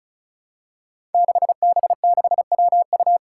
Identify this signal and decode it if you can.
Morse code, heard as a keyed tone, about 34 wpm, 700 Hz. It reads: 6B6WU
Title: Five Challenging Letter-Number Combinations 35wpm